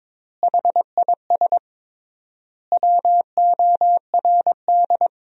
Morse code, heard as a keyed tone, about 22 wpm, 700 Hz. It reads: HIS WORD
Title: 3x Word-Spacing - Sentences from Top 300 Words - 22wpm